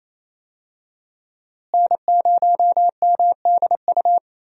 Morse code, heard as a keyed tone, about 28 wpm, 700 Hz. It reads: N0MDU
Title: USA Call Signs - 5 Characters - Repeated 3x - 28wpm